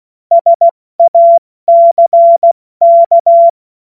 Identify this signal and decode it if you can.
Morse code, heard as a keyed tone, about 16 wpm, 680 Hz. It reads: SACK
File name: Morsekurs / Witze / Witz 16. November 2021, 70 F80